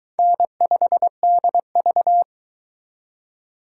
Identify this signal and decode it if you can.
Morse code, heard as a keyed tone, about 22 wpm, 700 Hz. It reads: N5DV